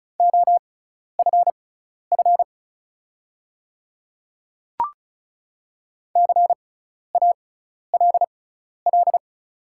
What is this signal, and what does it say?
Morse code, heard as a keyed tone, about 36 wpm, spaced as for 12 wpm, 700 Hz. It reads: OFF CALL
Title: 6x Character Spacing - Top 200 Words - 35wpm